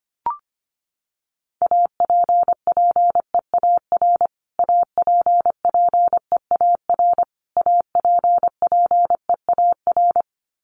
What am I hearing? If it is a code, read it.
Morse code, heard as a keyed tone, about 25 wpm, 700 Hz. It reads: APPEAR APPEAR APPEAR
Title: Top 500 Words Repeated 25wpm